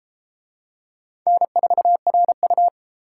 Morse code, heard as a keyed tone, about 32 wpm, 700 Hz. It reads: N4RU